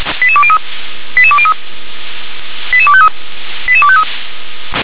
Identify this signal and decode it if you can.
Five-tone selective call (ZVEI), heard as quick runs of five tones, tones of about 70 ms. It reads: 88203 88203 88253 88253
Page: /357/SDR/ZVEI/